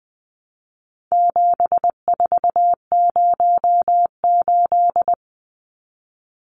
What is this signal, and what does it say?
Morse code, heard as a keyed tone, about 20 wpm, 700 Hz. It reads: 7408
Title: Numbers - 4-Digit 20wpm